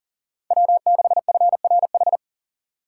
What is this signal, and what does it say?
Morse code, heard as a keyed tone, about 53 wpm, 700 Hz. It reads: W6FRH